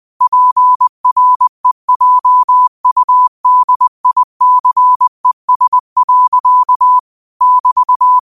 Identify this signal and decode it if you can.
Morse code, heard as a keyed tone, about 20 wpm, 1000 Hz. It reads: PREJUDICES. <BT>
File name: qotdfiles / 20wpm